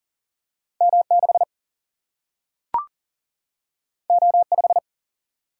Morse code, heard as a keyed tone, about 40 wpm, 700 Hz. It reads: M 6 O 5
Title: Single-Letter Single-Number 40wpm